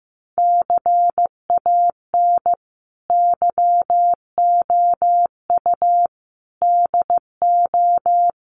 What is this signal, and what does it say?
Morse code, heard as a keyed tone, about 15 wpm, 700 Hz. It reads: CAN YOU DO
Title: Sentences from Top 100 Words 15wpm